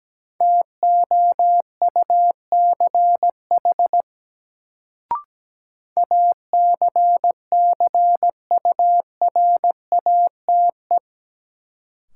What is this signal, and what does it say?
Morse code, heard as a keyed tone, about 17 wpm, 700 Hz. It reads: TOUCH ACCURATE
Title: Sentences from Top 1200 Words - New Word Drill - 17wpm